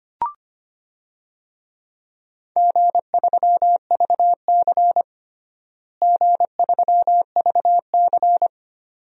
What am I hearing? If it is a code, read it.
Morse code, heard as a keyed tone, about 25 wpm, 700 Hz. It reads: G3VC G3VC
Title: DXCC-100 Call Signs - 4 Characters - Repeated 2x - 25wpm